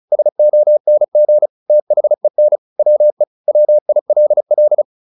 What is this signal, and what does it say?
Morse code, heard as a keyed tone, about 34 wpm, 600 Hz. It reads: SONG THEN WE WILL